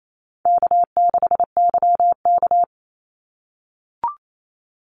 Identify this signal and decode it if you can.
Morse code, heard as a keyed tone, about 28 wpm, 700 Hz. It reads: K6YK
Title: NA SOTA Chaser Callsigns 28wpm